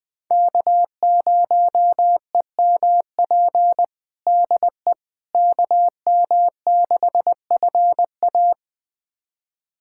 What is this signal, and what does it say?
Morse code, heard as a keyed tone, about 20 wpm, 700 Hz. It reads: K0EMP DE KM6FA
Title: USA Callsign DE Callsign - 5 Characters - 20wpm